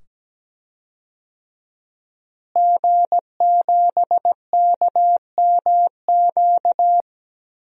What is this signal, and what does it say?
Morse code, heard as a keyed tone, about 17 wpm, 700 Hz. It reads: G7KMQ
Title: DXCC-100 Call Signs - 5 Characters - Repeated 2x - 17wpm